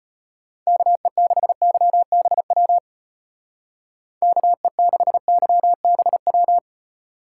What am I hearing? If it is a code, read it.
Morse code, heard as a keyed tone, about 33 wpm, 700 Hz. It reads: KE6YBW KE6YBW